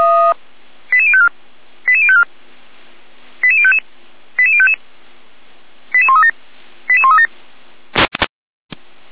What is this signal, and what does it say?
Five-tone selective call (ZVEI), heard as quick runs of five tones, tones of about 70 ms. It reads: 88054 88054 88055 88055 88137 88137